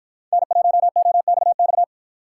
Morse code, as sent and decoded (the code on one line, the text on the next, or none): -. .---- --- -..- -..-
N1OXX